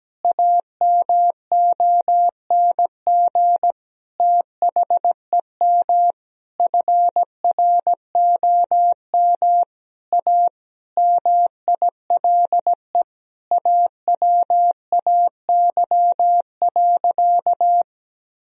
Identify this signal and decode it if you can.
Morse code, heard as a keyed tone, about 17 wpm, 700 Hz. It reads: AMONG THEM FROM A MILE AWAY.
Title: Sentences from Top 1000 Words - Review Part 3 - 17wpm